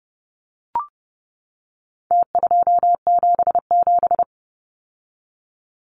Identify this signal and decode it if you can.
Morse code, heard as a keyed tone, about 29 wpm, 700 Hz. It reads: T277
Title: Numbers - Contest Style - 4-Digit 30wpm